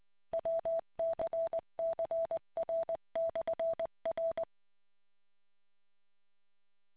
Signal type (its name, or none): Morse code